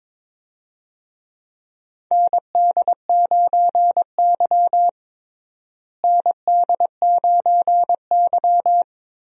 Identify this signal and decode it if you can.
Morse code, heard as a keyed tone, about 23 wpm, 700 Hz. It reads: ND9Y ND9Y
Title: USA Call Signs - 4 Characters - Repeated 3x - 22wpm